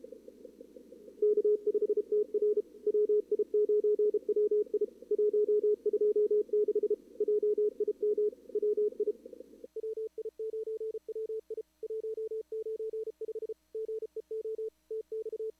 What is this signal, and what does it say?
Morse code, heard as a keyed tone, about 35 wpm, 440 Hz. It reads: K5TR WI9WI 126 JIM WI WI9WI 195 GEO TX
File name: CWSprint / Feb2022 / k5tr-feb-na-sprint-cw-k5ot -0100